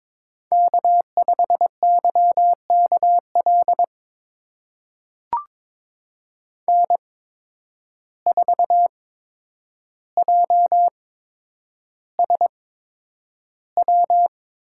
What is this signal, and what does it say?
Morse code, heard as a keyed tone, about 22 wpm, spaced as for 9 wpm, 700 Hz. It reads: K5YKL N4JSW
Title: USA Call Signs - 5 Characters - 8x Character Spacing - 22wpm